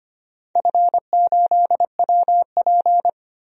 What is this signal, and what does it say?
Morse code, heard as a keyed tone, about 25 wpm, 700 Hz. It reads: F8WP
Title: DXCC-100 Call Signs - 4 Characters - 25wpm